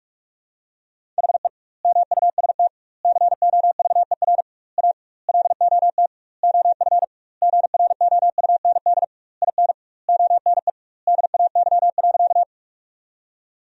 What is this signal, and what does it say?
Morse code, heard as a keyed tone, about 45 wpm, 700 Hz. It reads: HE MUST COVER A LOT OF GROUND IN ONE DAY.
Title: 2x Word-Spacing - Sentences from Top 500 Words - 45wpm